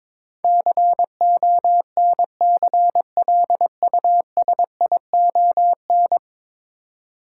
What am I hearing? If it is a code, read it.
Morse code, heard as a keyed tone, about 22 wpm, 700 Hz. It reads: CONCLUSION
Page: morse-code-ninja-practice-22wpm